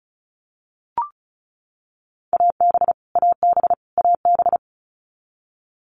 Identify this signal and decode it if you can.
Morse code, heard as a keyed tone, about 34 wpm, 700 Hz. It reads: AB AB AB
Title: Canadian Provinces and Territories - Repeated 3x - 35wpm